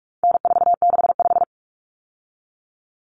Morse code, heard as a keyed tone, about 44 wpm, 700 Hz. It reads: N465
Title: Common Cut Numbers - 4-Digit 45wpm